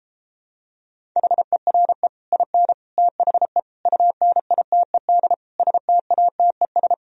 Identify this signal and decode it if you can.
Morse code, heard as a keyed tone, about 33 wpm, 700 Hz. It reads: HERE IN THE UNITED STATES